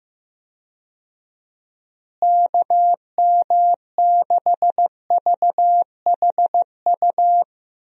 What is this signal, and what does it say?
Morse code, heard as a keyed tone, about 15 wpm, 700 Hz. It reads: KM6VHU
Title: USA Call Signs - 6 Characters - Repeated 3x - 15wpm